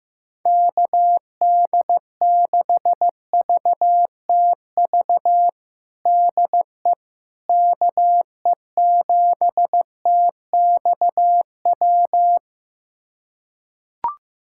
Morse code, heard as a keyed tone, about 15 wpm, 700 Hz. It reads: KD6VTV DE KE7TXW
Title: USA Callsign DE Callsign 15wpm